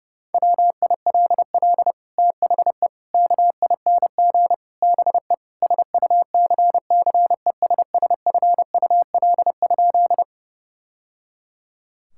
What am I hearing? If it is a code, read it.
Morse code, heard as a keyed tone, about 32 wpm, 700 Hz. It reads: WILL THE KING BE SUCCESSFUL?
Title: Sentences from Top 1000 Words - Review Part 6 - 30wpm